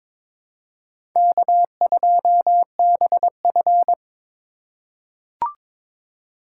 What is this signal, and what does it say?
Morse code, heard as a keyed tone, about 23 wpm, 700 Hz. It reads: K2BF
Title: USA Call Signs - 4 Characters - 4x Character Spacing - 22wpm